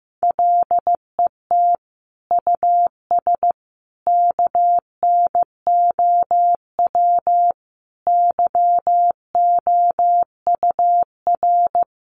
Morse code, heard as a keyed tone, about 14 wpm, 700 Hz. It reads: LET US KNOW YOUR
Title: Sentences from Top 1400 Words 15wpm